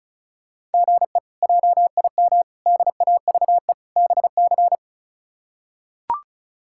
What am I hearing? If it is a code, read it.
Morse code, heard as a keyed tone, about 34 wpm, 700 Hz. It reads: GE JIM DAVE BC
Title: K1USN SST QSO Practice - 35wpm